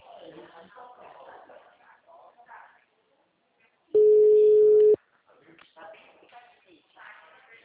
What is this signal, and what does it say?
Ringback tone (Europe, 425 Hz), with speech.